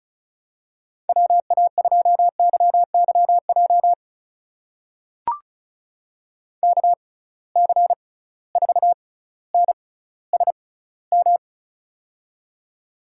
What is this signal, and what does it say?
Morse code, heard as a keyed tone, about 36 wpm, spaced as for 18 wpm, 700 Hz. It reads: WA2YYJ KC4NSM
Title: USA Call Signs - 6 Characters - 6x Character Spacing - 35wpm